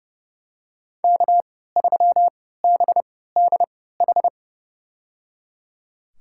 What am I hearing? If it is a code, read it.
Morse code, heard as a keyed tone, about 30 wpm, spaced as for 17 wpm, 700 Hz. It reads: K3BDH
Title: USA Call Signs - 3x Character Spacing - 30wpm